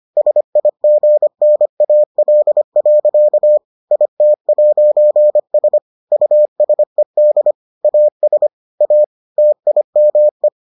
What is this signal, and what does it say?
Morse code, heard as a keyed tone, about 25 wpm, 600 Hz. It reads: SIGNAL. IT'S USED AS A TIME